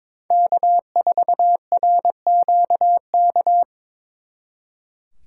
Morse code, heard as a keyed tone, about 26 wpm, 700 Hz. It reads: K4RQK
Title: Speed Racing - USA 1x3 Call Signs 22wpm